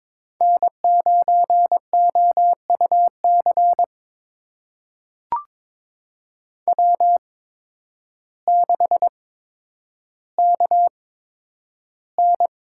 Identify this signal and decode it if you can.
Morse code, heard as a keyed tone, about 22 wpm, spaced as for 10 wpm, 700 Hz. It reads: N9OUC W6KN
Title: USA Call Signs - 5 Characters - 8x Character Spacing - 22wpm